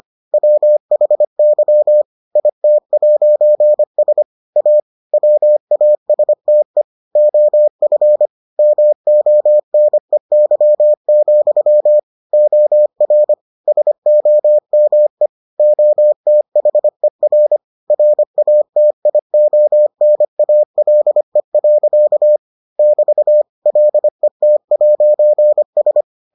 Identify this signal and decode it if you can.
Morse code, heard as a keyed tone, about 25 wpm, 600 Hz. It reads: WHY IT'S A WASTE OF MONEY, OR SOME OTHER RATIONALE. <BT> LET'S